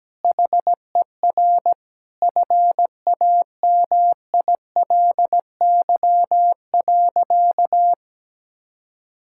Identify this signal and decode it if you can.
Morse code, heard as a keyed tone, about 17 wpm, 700 Hz. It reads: HER FAMILY.
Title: Sentences from Top 1000 Words - Review Part 5 - 17wpm